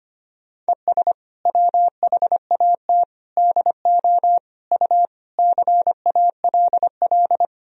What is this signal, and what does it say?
Morse code, heard as a keyed tone, about 25 wpm, 700 Hz. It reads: ES WHAT DO U CALL
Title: Sentences from Top 100 Words and Abbreviations 25wpm